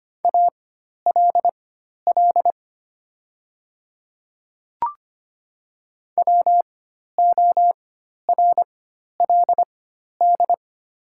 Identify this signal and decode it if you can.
Morse code, heard as a keyed tone, about 25 wpm, spaced as for 12 wpm, 690 Hz. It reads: ALL WORLD
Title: 4x Character Spacing - Top 200 Words - 25wpm